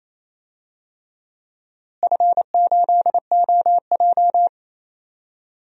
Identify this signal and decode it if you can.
Morse code, heard as a keyed tone, about 28 wpm, 700 Hz. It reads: F8OJ
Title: DXCC-100 Call Signs - 4 Characters - Repeated 3x - 28wpm